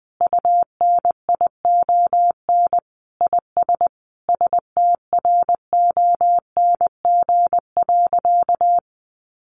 Morse code, heard as a keyed tone, about 20 wpm, 700 Hz. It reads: UNION IS STRONG.
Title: Sentences from Top 1400 Words 20wpm